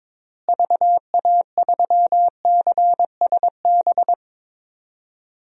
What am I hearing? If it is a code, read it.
Morse code, heard as a keyed tone, about 22 wpm, 700 Hz. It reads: VA3CSB
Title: DXCC-100 Call Signs - 6 Characters - Repeated 3x - 22wpm